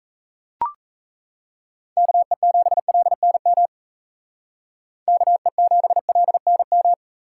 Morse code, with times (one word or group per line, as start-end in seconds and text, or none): 2.0-3.7 KE7LNM
5.1-6.9 KE7LNM